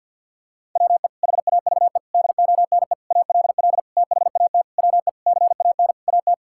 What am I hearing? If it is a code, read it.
Morse code, heard as a keyed tone, about 51 wpm, 700 Hz. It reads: WE HAVE DONE ALL THAT WE CAN AT
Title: Sentences from Top 1000 Words - Review Part 5 - 50wpm